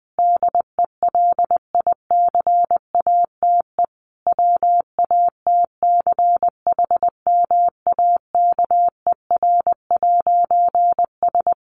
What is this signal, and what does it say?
Morse code, heard as a keyed tone, about 20 wpm, 700 Hz. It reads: DELICATE WATCHMAKER'S
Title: A Sound Of Thunder 20wpm